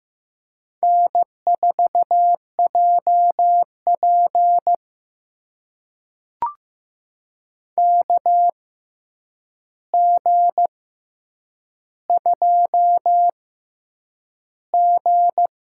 Morse code, heard as a keyed tone, about 15 wpm, spaced as for 8 wpm, 700 Hz. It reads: N4JP KG2G
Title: USA Call Signs - 4 Characters - 6x Character Spacing - 15wpm